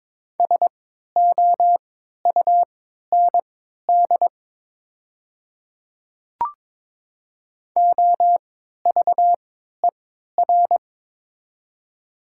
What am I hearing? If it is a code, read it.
Morse code, heard as a keyed tone, about 22 wpm, spaced as for 13 wpm, 700 Hz. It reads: SOUND OVER
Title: 3x Character Spacing - Top 100 Words - 22wpm